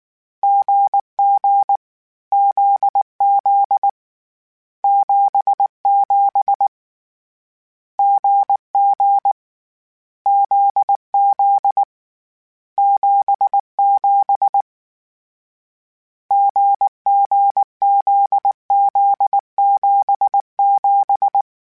Morse code, heard as a keyed tone, about 19 wpm, 800 Hz. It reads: GG ZZ 77 GG ZZ 77 GGZZ77
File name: CW Code Groups-mp3s / Sets mp3s / GZ GZ7 sp @18